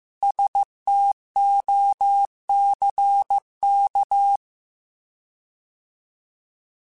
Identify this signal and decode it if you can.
Morse code, heard as a keyed tone, about 15 wpm, 800 Hz. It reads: STOCK
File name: Alphabetical / Alphabetical mp3s / Ss @13